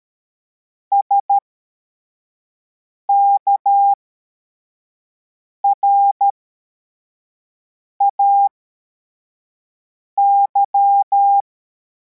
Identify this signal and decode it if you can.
Morse code, heard as a keyed tone, about 13 wpm, spaced as for 4 wpm, 800 Hz. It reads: SKRAY
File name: formation / cw / Text53